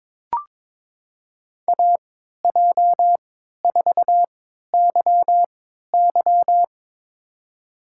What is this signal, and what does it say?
Morse code, heard as a keyed tone, about 22 wpm, spaced as for 13 wpm, 700 Hz. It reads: AJ4YY
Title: USA Call Signs - 5 Characters - 3x Character Spacing - 22wpm